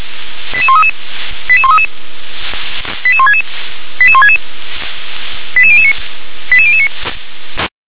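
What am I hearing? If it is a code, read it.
Five-tone selective call (ZVEI), heard as quick runs of five tones, tones of about 70 ms. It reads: 88133 88133 88166 88166 88009 88009
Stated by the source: ZVEI five-tone selective call received on SDR.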